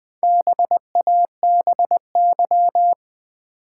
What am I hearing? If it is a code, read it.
Morse code, heard as a keyed tone, about 20 wpm, 700 Hz. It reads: BABY
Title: Sentences from Top 1300 Words - New Word Drill Alt - 20wpm